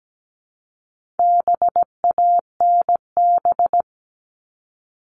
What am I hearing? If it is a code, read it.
Morse code, heard as a keyed tone, about 17 wpm, 700 Hz. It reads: BANB
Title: All Cut Numbers - 4-Digit 17wpm